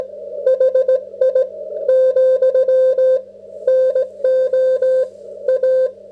Morse code, heard as a keyed tone, about 17 wpm, 530 Hz. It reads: HI , NO A